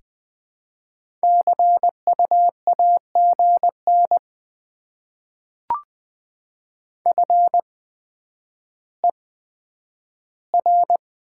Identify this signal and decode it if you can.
Morse code, heard as a keyed tone, about 20 wpm, spaced as for 10 wpm, 700 Hz. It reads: CU AGN FER
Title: Top 20 QSO Elements - 8x Character Spacing - 20wpm